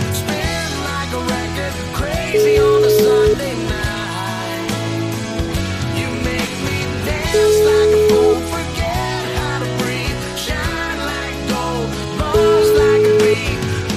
Ringback tone (Europe, 425 Hz), with noise.